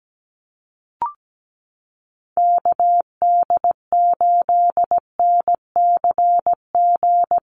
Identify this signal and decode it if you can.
Morse code, heard as a keyed tone, about 17 wpm, 700 Hz. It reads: KD8NCG
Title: USA Call Signs Repeated 3x 17wpm